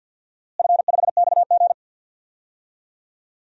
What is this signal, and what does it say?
Morse code, heard as a keyed tone, about 49 wpm, 700 Hz. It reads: F5XG
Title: DXCC-100 Call Signs - 4 Characters - Repeated 3x - 50wpm